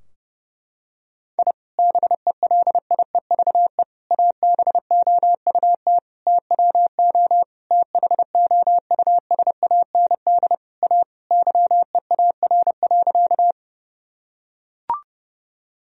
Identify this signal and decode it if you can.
Morse code, heard as a keyed tone, about 32 wpm, 700 Hz. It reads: I BELIEVE ABOUT TWO THOUSAND A YEAR.
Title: Sentences from Top 1000 Words - Review Part 3 - 30wpm